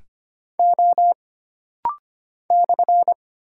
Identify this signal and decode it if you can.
Morse code, heard as a keyed tone, about 25 wpm, 700 Hz. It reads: O /
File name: Single Character (Letter:Number:Punctuation:Prosign) - Rapid-Fire - 25wpm